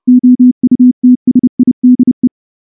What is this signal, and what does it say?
Morse code, heard as a keyed tone, about 29 wpm, 250 Hz. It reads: OUTSIDE
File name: morse / morse-training-30WPM